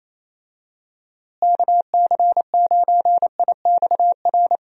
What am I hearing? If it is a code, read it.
Morse code, heard as a keyed tone, about 28 wpm, 700 Hz. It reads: KC9IXR
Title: USA Call Signs - 6 Characters - Repeated 3x - 28wpm